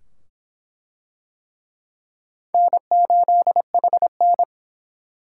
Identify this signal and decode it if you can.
Morse code, heard as a keyed tone, about 23 wpm, 700 Hz. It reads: N8HN